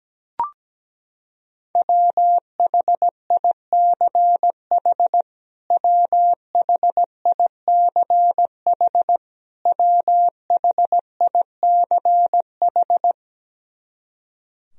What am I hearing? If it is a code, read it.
Morse code, heard as a keyed tone, about 17 wpm, 700 Hz. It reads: WHICH WHICH WHICH
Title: Top 100 Words Repeated 17wpm